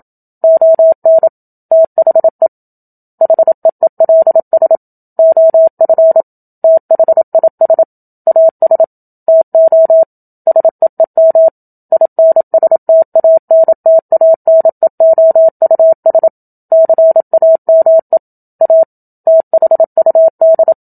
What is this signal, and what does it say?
Morse code, heard as a keyed tone, about 27 wpm, 650 Hz. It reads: ON THE HEELS OF THIS AS TO SEEM INSTANTANEOUS CAME A THUD
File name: Book Two / 01-Chapter 1 - 27 WPM